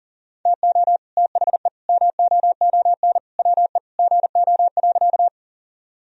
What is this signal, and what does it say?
Morse code, heard as a keyed tone, about 37 wpm, 700 Hz. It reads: TO THE MOON WE GO.
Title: Sentences from Top 1000 Words - Review Part 3 - 40wpm